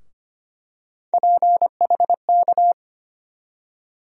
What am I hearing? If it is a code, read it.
Morse code, heard as a keyed tone, about 25 wpm, 700 Hz. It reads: PHK
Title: USA Call Sign Postfixes - 3 Letters 25wpm